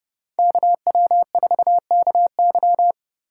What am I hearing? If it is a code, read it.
Morse code, heard as a keyed tone, about 34 wpm, 700 Hz. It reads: KW4KY